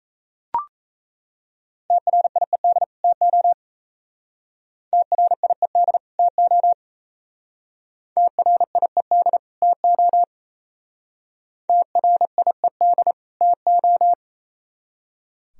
Morse code, heard as a keyed tone, about 38 wpm, 700 Hz. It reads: TRIED TO TRIED TO TRIED TO TRIED TO